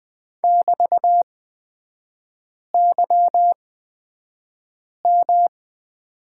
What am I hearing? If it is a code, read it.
Morse code, heard as a keyed tone, about 20 wpm, spaced as for 5 wpm, 700 Hz. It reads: <BT>YM
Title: 255 Lesson 41e - Period - Three Characters 20wpm